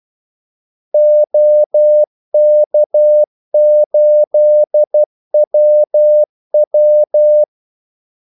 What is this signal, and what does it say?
Morse code, heard as a keyed tone, about 12 wpm, 600 Hz. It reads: OK8WW
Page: 2 ekstra ordmellemrum mellem kaldesignalerne